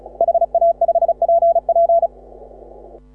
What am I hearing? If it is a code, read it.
Morse code, heard as a keyed tone, about 36 wpm, 680 Hz. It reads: HA5PP
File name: AUDIO2018 / HA6OI CW